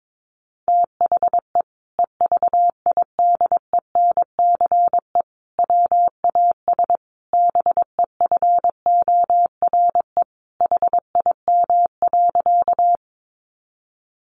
Morse code, heard as a keyed tone, about 22 wpm, 700 Hz. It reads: THE EVIDENCE WAS BEFORE HIM.
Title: Sentences from Top 1300 Words 22wpm